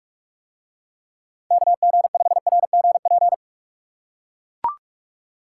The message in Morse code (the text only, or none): KG5RGP